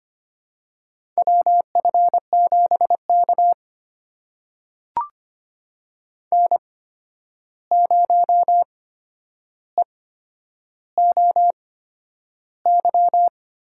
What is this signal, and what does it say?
Morse code, heard as a keyed tone, about 25 wpm, spaced as for 10 wpm, 700 Hz. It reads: WF7K N0EOY
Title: USA Call Signs - 8x Character Spacing - 25wpm